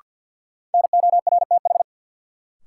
Morse code, heard as a keyed tone, about 53 wpm, 700 Hz. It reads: NORTH
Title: Sentences from Top 400 Words - New Word Drill - 50wpm